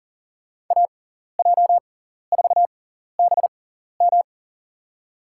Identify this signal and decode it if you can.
Morse code, heard as a keyed tone, about 42 wpm, spaced as for 14 wpm, 700 Hz. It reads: AJ4BM
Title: USA Call Signs - 5 Characters - 6x Character Spacing - 40wpm